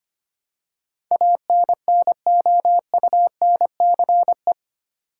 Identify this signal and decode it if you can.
Morse code, heard as a keyed tone, about 25 wpm, 700 Hz. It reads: ANNOUNCE
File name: 8 Letter Words - First 3 Letters Given - 25wpm